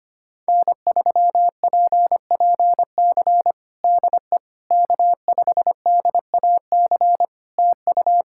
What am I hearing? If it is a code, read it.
Morse code, heard as a keyed tone, about 25 wpm, 700 Hz. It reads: N3PPC DE K5DAC TU
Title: K1USN SST QSO Practice 25wpm